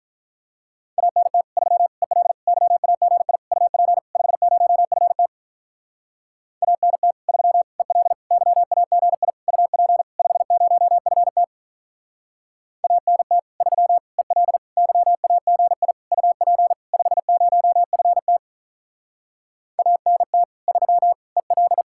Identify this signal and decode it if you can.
Morse code, heard as a keyed tone, about 43 wpm, 700 Hz. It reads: ANT 3 EL YAGI UP 50FT ANT 3 EL YAGI UP 50FT ANT 3 EL YAGI UP 50FT ANT 3 EL
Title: Speed-Racing Common QSO Phrases - 35wpm